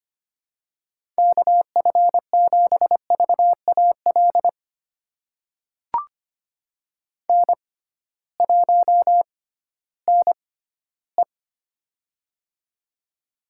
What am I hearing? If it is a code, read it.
Morse code, heard as a keyed tone, about 25 wpm, spaced as for 15 wpm, 700 Hz. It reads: KF7VAL N1NE
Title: USA Call Signs - 6x Character Spacing - 25wpm